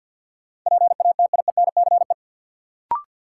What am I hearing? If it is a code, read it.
Morse code, heard as a keyed tone, about 53 wpm, 700 Hz. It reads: PATIENCE
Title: Sentences from Top 1300 Words - New Word Drill - 50wpm